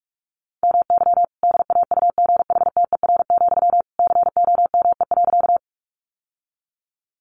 Morse code, heard as a keyed tone, about 46 wpm, 700 Hz. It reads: MY DAUGHTER, COME.
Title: Sentences from Top 1500 Words 45wpm